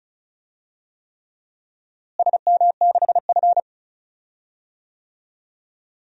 Morse code, heard as a keyed tone, about 34 wpm, 700 Hz. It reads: SM6F
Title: DXCC-100 Call Signs - 4 Characters - Repeated 3x - 35wpm